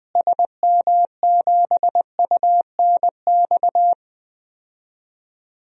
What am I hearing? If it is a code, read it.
Morse code, heard as a keyed tone, about 20 wpm, 700 Hz. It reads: SM7UNX
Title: DXCC-100 Call Signs - 6 Characters - Repeated 3x - 20wpm